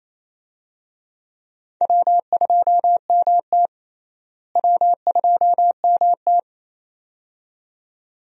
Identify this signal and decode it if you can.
Morse code, heard as a keyed tone, about 28 wpm, 700 Hz. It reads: W2MT W2MT
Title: USA Call Signs - 4 Characters - Repeated 2x - 28wpm